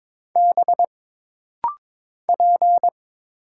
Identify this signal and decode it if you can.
Morse code, heard as a keyed tone, about 22 wpm, 700 Hz. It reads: B P
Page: This rapid-fire timing is geared to push your ICR (Instant Character Recognition) skills.